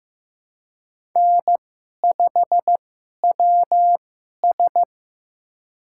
Morse code, heard as a keyed tone, about 15 wpm, spaced as for 11 wpm, 690 Hz. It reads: N5WS
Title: USA Call Signs - 4 Characters - 2x Character Spacing - 15wpm